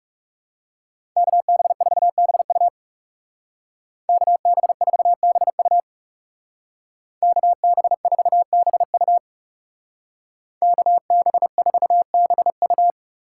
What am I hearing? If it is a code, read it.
Morse code, heard as a keyed tone, about 37 wpm, 700 Hz. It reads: KB4BU KB4BU KB4BU KB4BU
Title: Speed Racing - USA 2x2 Call Signs 30wpm